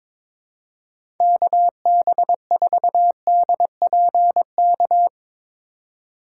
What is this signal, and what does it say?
Morse code, heard as a keyed tone, about 22 wpm, 700 Hz. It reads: KB4DPK
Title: USA Call Signs - 6 Characters - 2x Character Spacing - 22wpm